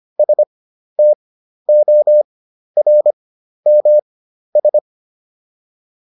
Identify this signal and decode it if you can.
Morse code, heard as a keyed tone, about 25 wpm, spaced as for 12 wpm, 600 Hz. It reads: STORMS
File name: SS017 12